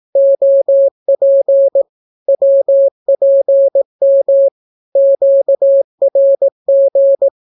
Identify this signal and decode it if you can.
Morse code, heard as a keyed tone, about 18 wpm, 550 Hz. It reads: OP WPM QRG
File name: naukapodstaw / skroty / skroty